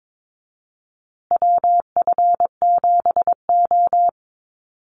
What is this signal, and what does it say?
Morse code, heard as a keyed tone, about 22 wpm, 700 Hz. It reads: WF7O
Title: USA Call Signs Repeated 2x 22wpm